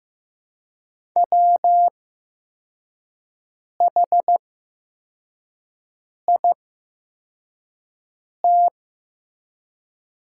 Morse code, heard as a keyed tone, about 15 wpm, spaced as for 4 wpm, 700 Hz. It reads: WHIT
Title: 8x Character Spacing - Top 300 Words - 15wpm